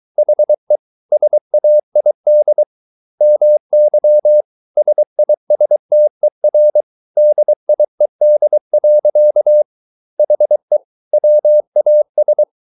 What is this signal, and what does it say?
Morse code, heard as a keyed tone, about 23 wpm, 600 Hz. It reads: HE SAID MY SISTER DIED. HE WAS